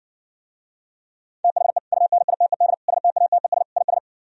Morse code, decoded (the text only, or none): THE UNITED STATES IS